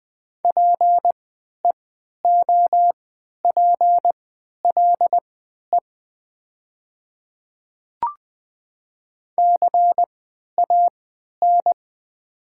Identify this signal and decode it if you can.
Morse code, heard as a keyed tone, about 20 wpm, spaced as for 11 wpm, 700 Hz. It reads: PEOPLE CAN
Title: 3x Character Spacing - Top 100 Words - 20wpm